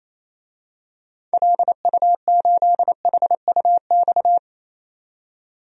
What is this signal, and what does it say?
Morse code, heard as a keyed tone, about 28 wpm, 700 Hz. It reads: LU8HUX
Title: DXCC-100 Call Signs - 6 Characters - 28wpm